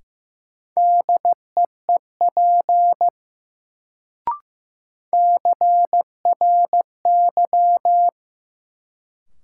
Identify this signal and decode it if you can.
Morse code, heard as a keyed tone, about 15 wpm, 700 Hz. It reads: DEEP CRY
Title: Sets of 5 Words Encore - New Word Drill - 15wpm